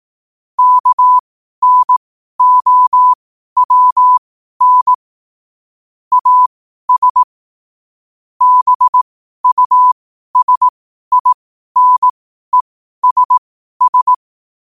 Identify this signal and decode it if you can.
Morse code, heard as a keyed tone, about 18 wpm, spaced as for 12 wpm, 1000 Hz. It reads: KNOWN AS BUSINESS
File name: qotdfiles / 12wpm